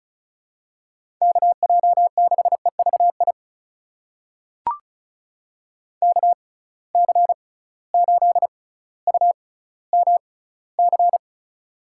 Morse code, heard as a keyed tone, about 34 wpm, spaced as for 18 wpm, 700 Hz. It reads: KJ6EVI KC8UMC
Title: USA Call Signs - 6 Characters - 6x Character Spacing - 35wpm